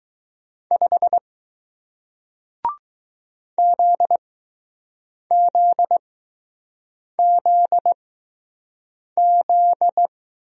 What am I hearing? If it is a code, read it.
Morse code, heard as a keyed tone, about 20 wpm, spaced as for 6 wpm, 700 Hz. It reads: 5 ZZZZ